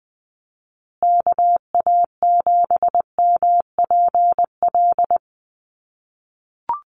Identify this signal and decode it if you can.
Morse code, heard as a keyed tone, about 20 wpm, 700 Hz. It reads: KA7MPL
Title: USA Call Signs Repeated 2x 20wpm